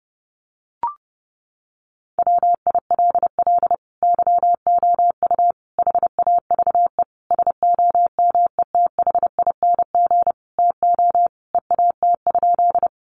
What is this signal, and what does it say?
Morse code, heard as a keyed tone, about 30 wpm, 700 Hz. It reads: WILL YOU HAVE SOMETHING TO EAT?
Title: Sentences from Top 600 Words 30wpm